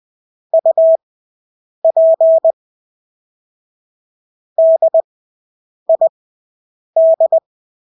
Morse code, heard as a keyed tone, about 20 wpm, spaced as for 8 wpm, 650 Hz. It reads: UP DID
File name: Engelske ord 20/8 wpm